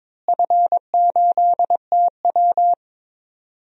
Morse code, heard as a keyed tone, about 22 wpm, 700 Hz. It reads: F8TW
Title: DXCC-100 Call Signs - 4 Characters - Repeated 2x - 22wpm